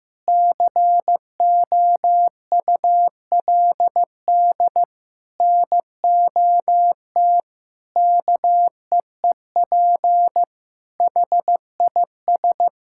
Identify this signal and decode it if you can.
Morse code, heard as a keyed tone, about 15 wpm, 700 Hz. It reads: COULD NOT KEEP HIS
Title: Sentences from Top 1000 Words - Review Part 6 - 15wpm